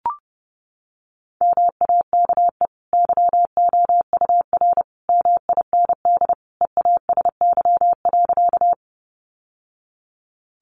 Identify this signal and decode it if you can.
Morse code, heard as a keyed tone, about 30 wpm, 700 Hz. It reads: MAKE YOUR MIND EASY.
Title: Sentences from Top 700 Words 30wpm